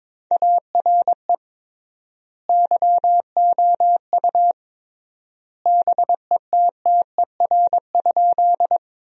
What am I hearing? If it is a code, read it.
Morse code, heard as a keyed tone, about 23 wpm, 700 Hz. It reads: ARE YOU BETTER?
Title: 3x Word-Spacing - Sentences from Top 400 Words - 22wpm